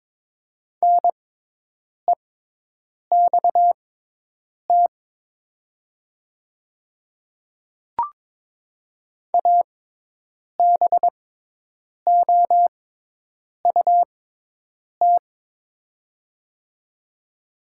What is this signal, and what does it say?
Morse code, heard as a keyed tone, about 22 wpm, spaced as for 8 wpm, 700 Hz. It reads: NEXT ABOUT
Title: 6x Character Spacing - Top 300 Words - 22wpm